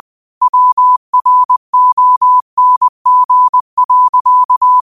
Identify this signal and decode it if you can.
Morse code, heard as a keyed tone, about 20 wpm, 1000 Hz. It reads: WRONG.